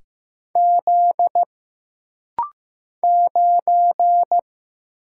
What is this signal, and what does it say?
Morse code, heard as a keyed tone, about 15 wpm, 700 Hz. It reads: Z 9
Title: Single Letter-Number Alt - Rapid-Fire 15wpm